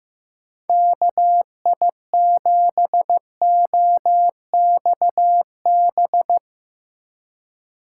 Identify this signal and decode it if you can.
Morse code, heard as a keyed tone, about 15 wpm, 700 Hz. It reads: KI7OXB
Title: USA Call Signs - 6 Characters - 6x Character Spacing - 15wpm